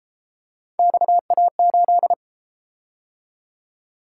Morse code, heard as a keyed tone, about 30 wpm, 700 Hz. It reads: XA8